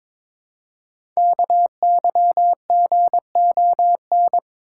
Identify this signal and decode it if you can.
Morse code, heard as a keyed tone, about 23 wpm, 700 Hz. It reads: KYGON
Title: Code Groups - 5 Letters - 22wpm